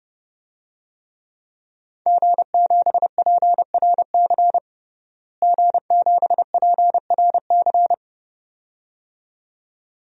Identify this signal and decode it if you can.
Morse code, heard as a keyed tone, about 32 wpm, 700 Hz. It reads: G7PRC G7PRC
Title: DXCC-100 Call Signs - 5 Characters - Repeated 2x - 30wpm